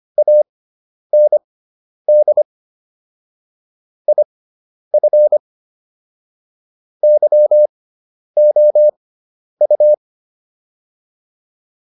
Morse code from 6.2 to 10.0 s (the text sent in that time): YOU